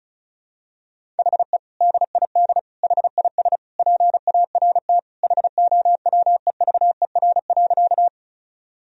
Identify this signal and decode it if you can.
Morse code, heard as a keyed tone, about 34 wpm, 700 Hz. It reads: HE DID HIS PART HOWEVER.
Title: Sentences from Top 1000 Words - Review Part 1 - 35wpm